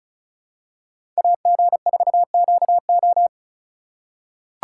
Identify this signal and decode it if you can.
Morse code, heard as a keyed tone, about 36 wpm, 700 Hz. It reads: AG4QO
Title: USA Call Signs - 5 Characters - 2x Character Spacing - 35wpm